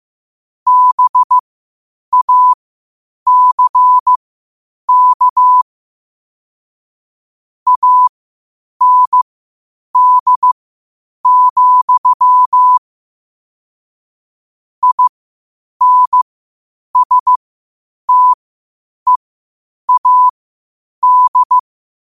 Morse code, heard as a keyed tone, about 15 wpm, spaced as for 8 wpm, 1000 Hz. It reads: BACK AND, INSTEAD
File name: qotdfiles / 8wpm